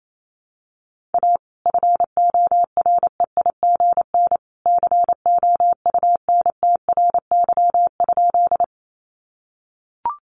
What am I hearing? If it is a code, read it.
Morse code, heard as a keyed tone, about 28 wpm, 700 Hz. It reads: A FOREIGN COUNTRY?
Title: Sentences from Top 1400 Words 28wpm